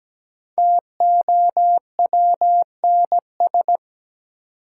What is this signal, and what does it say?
Morse code, heard as a keyed tone, about 17 wpm, 700 Hz. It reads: TOWNS
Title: Sentences from Top 1000 Words - Review Part 6 - New Word Drill - 17wpm